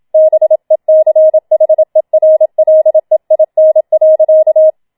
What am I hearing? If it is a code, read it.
Morse code, heard as a keyed tone, about 27 wpm, 630 Hz. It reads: BECHERLEIN.